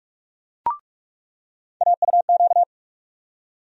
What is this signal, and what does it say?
Morse code, heard as a keyed tone, about 46 wpm, 700 Hz. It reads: AUQ
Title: USA Call Sign Postfixes - 3 Letters 45wpm